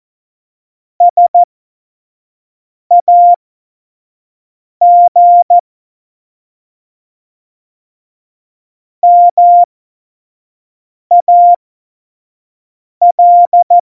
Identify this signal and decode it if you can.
Morse code, heard as a keyed tone, about 14 wpm, spaced as for 5 wpm, 700 Hz. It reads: SAG MAL
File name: Morsekurs / Witze / Witz 30. Juli 2019, 25 F70